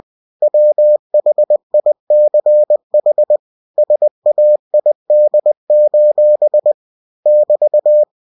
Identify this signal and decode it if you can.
Morse code, heard as a keyed tone, about 20 wpm, 600 Hz. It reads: WHICH SAID: <BT>
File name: cw-books / winnie-the-pooh / winnie-20wpm-page-0029